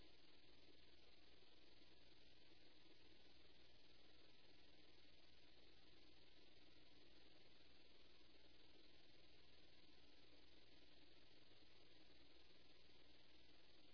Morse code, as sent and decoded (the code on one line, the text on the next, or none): none